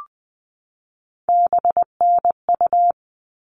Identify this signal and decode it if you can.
Morse code, heard as a keyed tone, about 20 wpm, 700 Hz. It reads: BNU
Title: All Cut Numbers - 3-Digit 20wpm